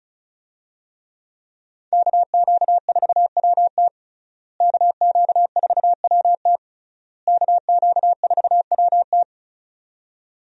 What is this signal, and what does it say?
Morse code, heard as a keyed tone, about 34 wpm, 700 Hz. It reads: KQ4WT KQ4WT KQ4WT
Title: USA Call Signs - 5 Characters - Repeated 3x - 35wpm